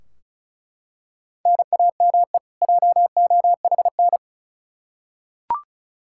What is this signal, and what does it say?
Morse code, heard as a keyed tone, about 36 wpm, 700 Hz. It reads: NAME JOHN
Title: Practice QSO - Exchange 1 - Beginning - 35wpm